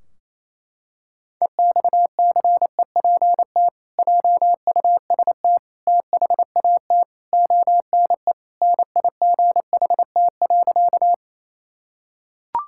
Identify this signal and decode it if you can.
Morse code, heard as a keyed tone, about 28 wpm, 700 Hz. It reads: EXCEPT JUST THAT ONE NIGHT.
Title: Sentences from Top 1600 Words 28wpm